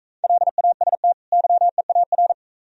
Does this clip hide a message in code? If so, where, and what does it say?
Yes: Morse code from 0.2 to 2.3 s, heard as a keyed tone, about 42 wpm, 690 Hz. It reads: LAST YEAR